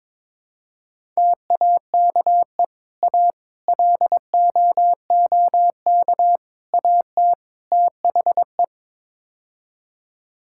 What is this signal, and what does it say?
Morse code, heard as a keyed tone, about 22 wpm, 700 Hz. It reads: TAKE A LOOK AT THE
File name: practice / 22wpm / Build A Phrase - 5-Word Phrases - 22wpm